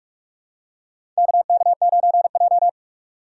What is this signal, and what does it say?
Morse code, heard as a keyed tone, about 46 wpm, 700 Hz. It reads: KK9J
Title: USA Call Signs - 4 Characters - 3x Character Spacing - 45wpm